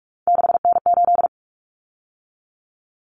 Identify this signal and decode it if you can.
Morse code, heard as a keyed tone, about 44 wpm, 700 Hz. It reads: 6N8
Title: Common Cut Numbers - 3-Digit 45wpm